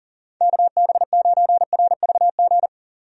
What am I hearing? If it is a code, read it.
Morse code, heard as a keyed tone, about 37 wpm, 700 Hz. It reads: KB9RVG
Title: USA Call Signs - 6 Characters - Repeated 2x - 40wpm